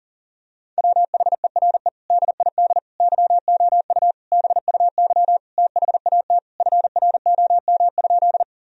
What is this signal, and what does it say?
Morse code, heard as a keyed tone, about 40 wpm, 700 Hz. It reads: WHERE DID YOU BUY THAT FROM?
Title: Sentences from Top 1000 Words - Review Part 4 - 40wpm